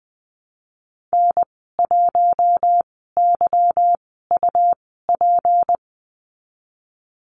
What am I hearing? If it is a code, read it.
Morse code, heard as a keyed tone, about 20 wpm, spaced as for 15 wpm, 700 Hz. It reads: N1YUP
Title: 167 Lesson 27g - Number 7 - Call Signs 20wpm